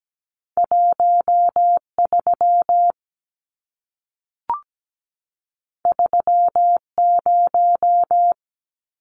Morse code, heard as a keyed tone, about 17 wpm, 700 Hz. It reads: 13 30
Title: Numbers - 2-Digit 17wpm